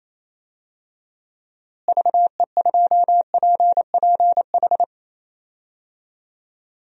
Morse code, heard as a keyed tone, about 28 wpm, 700 Hz. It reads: VE2PPH
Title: DXCC-100 Call Signs - 6 Characters - Repeated 3x - 28wpm